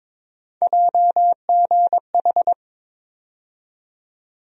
Morse code, heard as a keyed tone, about 22 wpm, 700 Hz. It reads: JGH
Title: USA Call Sign Postfixes - 3 Letters 22wpm